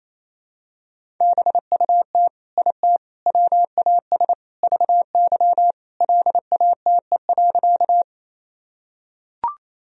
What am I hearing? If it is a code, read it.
Morse code, heard as a keyed tone, about 27 wpm, 700 Hz. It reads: BUT IT WAS VY LATE.
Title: Sentences from Top 300 Words and Abbreviations 28wpm